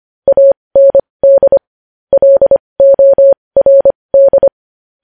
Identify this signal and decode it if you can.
Morse code, heard as a keyed tone, about 25 wpm, spaced as for 20 wpm, 550 Hz. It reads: AND LORD
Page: news headlines in morse code at 20 wpm